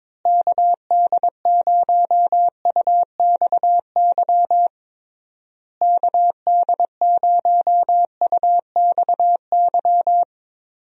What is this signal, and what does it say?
Morse code, heard as a keyed tone, about 22 wpm, 700 Hz. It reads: KD0UXY KD0UXY
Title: USA Call Signs - 6 Characters - Repeated 3x - 22wpm